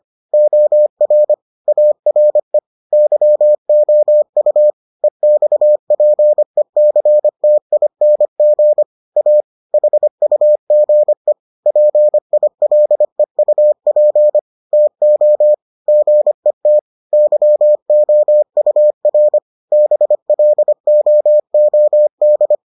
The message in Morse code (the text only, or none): OR ARE YOU EXPECTING A HUGE PILEUP TO GET YOUR BLOOD